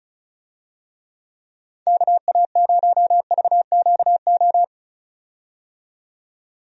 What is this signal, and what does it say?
Morse code, heard as a keyed tone, about 36 wpm, 700 Hz. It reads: KA0VQO
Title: USA Call Signs - 6 Characters - 35wpm